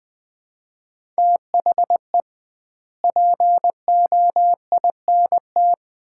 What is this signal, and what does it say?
Morse code, heard as a keyed tone, about 20 wpm, 700 Hz. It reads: THE POINT
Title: 2x Word-Spacing - Sets of 5 Words - 20wpm